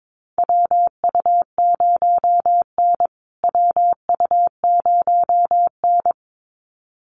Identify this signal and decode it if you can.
Morse code, heard as a keyed tone, about 22 wpm, 700 Hz. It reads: WU0N WU0N
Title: USA Call Signs Repeated 2x 22wpm